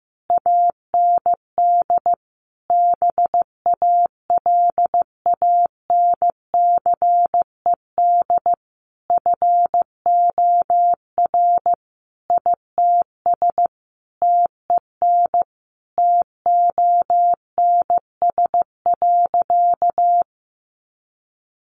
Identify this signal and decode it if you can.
Morse code, heard as a keyed tone, about 15 wpm, 700 Hz. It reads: AND BALANCED FOR ITS TEN TONS.
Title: A Sound Of Thunder 15wpm